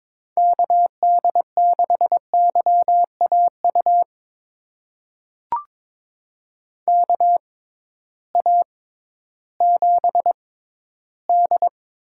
Morse code, heard as a keyed tone, about 22 wpm, spaced as for 13 wpm, 700 Hz. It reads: KD6YAU KA7D
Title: USA Call Signs - 6 Characters - 6x Character Spacing - 22wpm